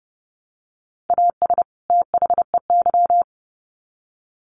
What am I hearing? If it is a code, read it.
Morse code, heard as a keyed tone, about 30 wpm, 700 Hz. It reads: AS THEY
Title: Sets of 2 Words 30wpm